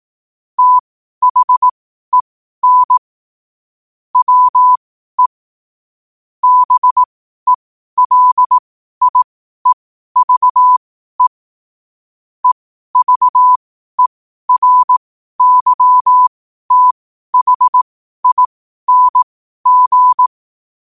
Morse code, heard as a keyed tone, about 18 wpm, spaced as for 12 wpm, 1000 Hz. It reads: THEN WE BELIEVE EVERYTHING
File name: qotdfiles / 12wpm